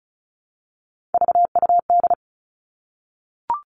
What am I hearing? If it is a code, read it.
Morse code, heard as a keyed tone, about 36 wpm, 700 Hz. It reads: VUD
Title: Three Challenging Letter-Number Combinations 35wpm